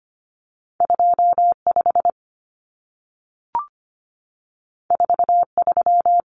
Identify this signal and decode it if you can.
Morse code, heard as a keyed tone, about 25 wpm, 700 Hz. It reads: 25 43
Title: Common Cut Numbers - 2-Digit 25wpm